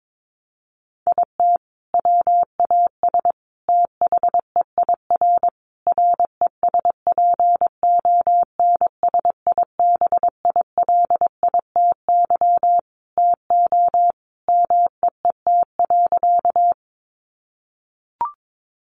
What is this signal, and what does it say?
Morse code, heard as a keyed tone, about 22 wpm, 700 Hz. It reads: IT WAS THEIR RESPONSIBILITY TO MEET.
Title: Sentences from Top 1200 Words 22wpm